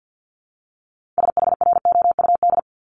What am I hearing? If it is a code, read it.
Morse code, heard as a keyed tone, about 50 wpm, 700 Hz. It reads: SHROUD
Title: Multi-letter Phonemes 50wpm